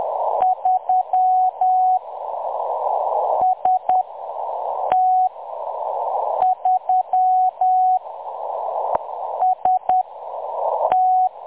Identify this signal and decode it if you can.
Morse code, heard as a keyed tone, about 10 wpm, 740 Hz. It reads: 3 S T 3 S T
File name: M23-ST3- -0858utc-10184khz